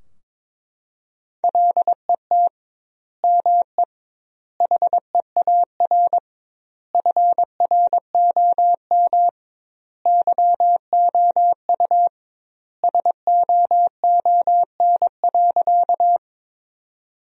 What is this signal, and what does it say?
Morse code, heard as a keyed tone, about 21 wpm, 700 Hz. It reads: LET ME HEAR FROM YOU SOON.
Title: 2x Word-Spacing - Sentences from Top 400 Words - 22wpm